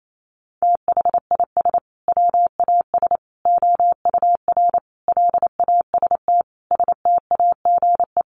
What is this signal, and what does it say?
Morse code, heard as a keyed tone, about 28 wpm, 700 Hz. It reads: THIS WAS OUR LAST STAGE
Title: Sentences from Top 1200 Words 28wpm